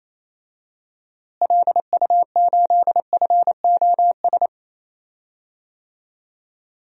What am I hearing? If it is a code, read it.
Morse code, heard as a keyed tone, about 28 wpm, 700 Hz. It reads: LU8FOS
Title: DXCC-100 Call Signs - 6 Characters - Repeated 3x - 28wpm